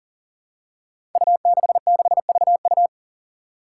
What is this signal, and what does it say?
Morse code, heard as a keyed tone, about 39 wpm, 700 Hz. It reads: U66VU
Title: Five Challenging Letter-Number Combinations 40wpm